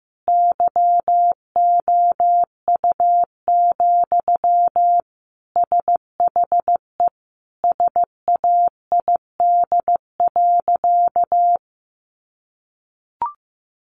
Morse code, heard as a keyed tone, about 15 wpm, 700 Hz. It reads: YOU, SHE SAID.
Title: Sentences from Top 600 Words 15wpm